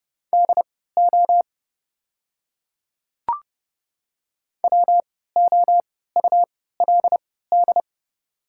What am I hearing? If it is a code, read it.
Morse code, heard as a keyed tone, about 30 wpm, spaced as for 17 wpm, 700 Hz. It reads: DO WOULD
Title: 3x Character Spacing - Top 100 Words - 30wpm